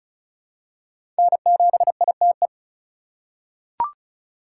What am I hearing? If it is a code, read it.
Morse code, heard as a keyed tone, about 34 wpm, 700 Hz. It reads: N7ITE
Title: USA Call Signs - 3x Character Spacing - 35wpm